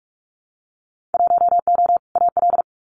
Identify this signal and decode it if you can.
Morse code, heard as a keyed tone, about 46 wpm, 700 Hz. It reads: 1O AL
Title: Winter Field Day Exchanges 45wpm